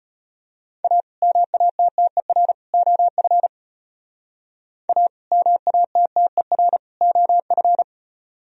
Morse code, heard as a keyed tone, about 32 wpm, 700 Hz. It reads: A MATTER OF A MATTER OF